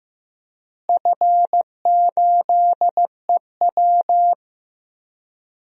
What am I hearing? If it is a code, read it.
Morse code, heard as a keyed tone, about 15 wpm, 700 Hz. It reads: F8EW
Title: DXCC-100 Call Signs - 4 Characters - Repeated 3x - 15wpm